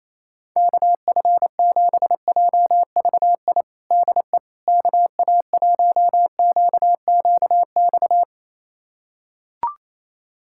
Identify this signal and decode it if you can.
Morse code, heard as a keyed tone, about 28 wpm, 700 Hz. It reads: KF7JVI DE KA1QQX
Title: USA Callsign DE Callsign - 6 Characters - 28wpm